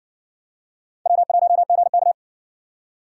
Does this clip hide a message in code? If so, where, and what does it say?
Yes: Morse code from 1.1 to 2.1 s, heard as a keyed tone, about 57 wpm, 700 Hz. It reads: W1GK